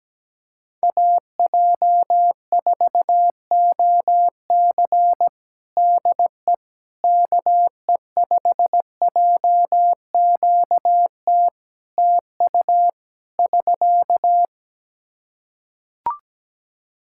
Morse code, heard as a keyed tone, about 17 wpm, 700 Hz. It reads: AJ4OC DE KE5JQT TU <SK>
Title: K1USN SST QSO Practice 17wpm